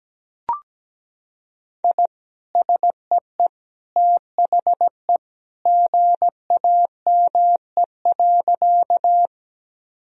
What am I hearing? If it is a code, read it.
Morse code, heard as a keyed tone, about 17 wpm, 700 Hz. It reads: I SEE THE GAME.
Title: Sentences from Top 1000 Words - Review Part 3 - 17wpm